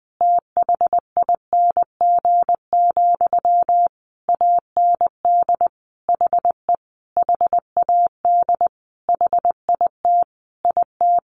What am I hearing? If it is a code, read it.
Morse code, heard as a keyed tone, about 20 wpm, 700 Hz. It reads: THING, AND HE HAD HIT IT
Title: Sentences from Top 1200 Words 20wpm